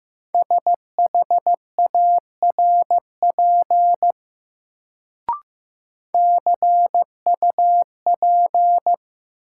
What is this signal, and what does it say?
Morse code, heard as a keyed tone, about 15 wpm, 700 Hz. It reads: SHARP CUP
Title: Sentences from Top 1600 Words - New Word Drill Alt - 15wpm